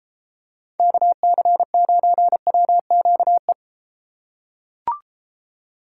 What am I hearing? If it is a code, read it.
Morse code, heard as a keyed tone, about 33 wpm, 700 Hz. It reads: KC9WQE